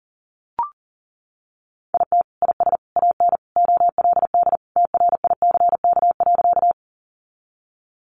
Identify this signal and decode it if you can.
Morse code, heard as a keyed tone, about 40 wpm, 700 Hz. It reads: IT IS AN OLD TRICK.
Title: Sentences from Top 1500 Words 40wpm